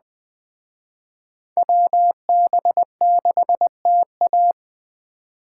Morse code, heard as a keyed tone, about 21 wpm, 700 Hz. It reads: WB6TA